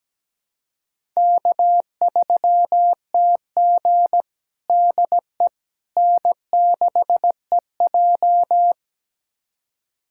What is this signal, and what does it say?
Morse code, heard as a keyed tone, about 17 wpm, 700 Hz. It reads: K3TG DE N6EJ
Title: USA Callsign DE Callsign - 4 Characters - 17wpm